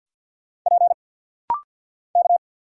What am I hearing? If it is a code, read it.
Morse code, heard as a keyed tone, about 50 wpm, 700 Hz. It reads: P K
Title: Single Letter-Number Alt - Rapid-Fire 50wpm